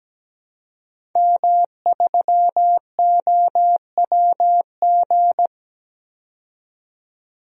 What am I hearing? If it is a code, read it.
Morse code, heard as a keyed tone, about 17 wpm, 700 Hz. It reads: M3OWG
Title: DXCC-100 Call Signs - 5 Characters - 17wpm